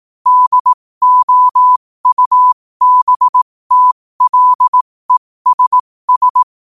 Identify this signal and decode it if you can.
Morse code, heard as a keyed tone, about 18 wpm, spaced as for 15 wpm, 1000 Hz. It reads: DOUBTLESS
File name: qotdfiles / 15wpm